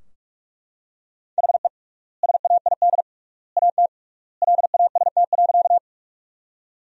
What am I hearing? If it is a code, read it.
Morse code, heard as a keyed tone, about 42 wpm, 700 Hz. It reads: HE SAID AT LAST.
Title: 3x Word-Spacing - Sentences from Top 300 Words - 45wpm